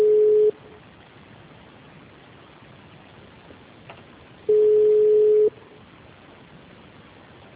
Ringback tone (Europe, 425 Hz).